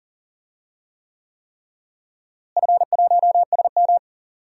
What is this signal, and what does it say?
Morse code, heard as a keyed tone, about 40 wpm, 700 Hz. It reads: F1SM
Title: DXCC-100 Call Signs - 4 Characters - Repeated 3x - 40wpm